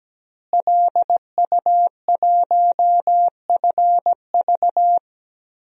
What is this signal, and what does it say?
Morse code, heard as a keyed tone, about 17 wpm, 700 Hz. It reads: LU1FV
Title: DXCC-100 Call Signs - 5 Characters - Repeated 3x - 17wpm